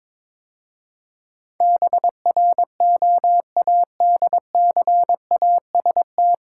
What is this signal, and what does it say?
Morse code, heard as a keyed tone, about 22 wpm, 700 Hz. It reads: BROADCAST
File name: 9 Letter Words - First 3 Letters Given - 22wpm